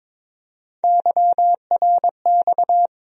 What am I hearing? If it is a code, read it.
Morse code, heard as a keyed tone, about 22 wpm, 700 Hz. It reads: YRX
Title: USA Call Sign Postfixes - 3 Letters 22wpm